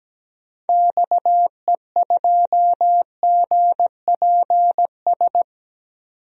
Morse code, heard as a keyed tone, about 17 wpm, 700 Hz. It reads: XE2GPS
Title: DXCC-100 Call Signs - 6 Characters - 17wpm